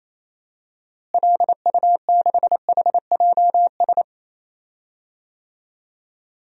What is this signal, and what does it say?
Morse code, heard as a keyed tone, about 28 wpm, 700 Hz. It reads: LU6HJS
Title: DXCC-100 Call Signs - 6 Characters - Repeated 3x - 28wpm